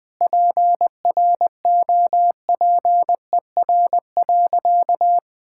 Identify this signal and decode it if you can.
Morse code, heard as a keyed tone, about 20 wpm, 700 Hz. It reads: PROPER.
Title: Sentences from Top 1000 Words - Review Part 5 - 20wpm